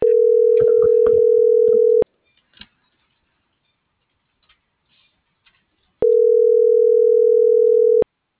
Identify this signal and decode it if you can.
Ringback tone (North America), 440 + 480 Hz.